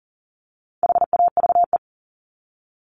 Morse code, heard as a keyed tone, about 40 wpm, 700 Hz. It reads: HAVE
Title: Top 200 Words Repeated 40wpm